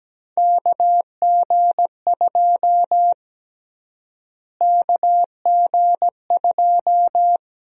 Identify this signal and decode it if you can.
Morse code, heard as a keyed tone, about 17 wpm, 700 Hz. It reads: KG2 KG2
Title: USA Call Sign Prefixes - Repeated 2x - 17wpm